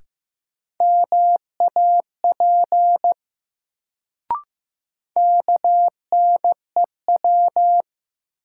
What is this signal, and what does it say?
Morse code, heard as a keyed tone, about 15 wpm, 700 Hz. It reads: MAP KNEW
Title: Sentences from Top 400 Words - New Word Drill - 15wpm